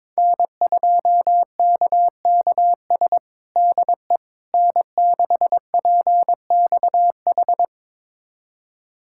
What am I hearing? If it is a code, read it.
Morse code, heard as a keyed tone, about 22 wpm, 700 Hz. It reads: N2KKS DE N6PXH
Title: USA Callsign DE Callsign - 5 Characters - 22wpm